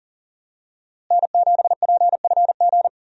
Morse code, heard as a keyed tone, about 40 wpm, 700 Hz. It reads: N7PFG
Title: USA Call Signs - 5 Characters - Repeated 2x - 40wpm